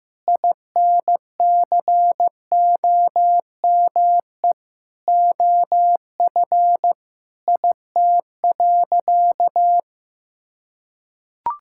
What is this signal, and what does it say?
Morse code, heard as a keyed tone, about 15 wpm, 700 Hz. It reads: INCOME OF IT.
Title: Sentences from Top 1000 Words - Review Part 6 - 15wpm